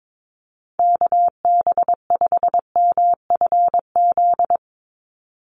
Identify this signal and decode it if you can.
Morse code, heard as a keyed tone, about 22 wpm, 700 Hz. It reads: KB5MFZ
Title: USA Call Signs 22wpm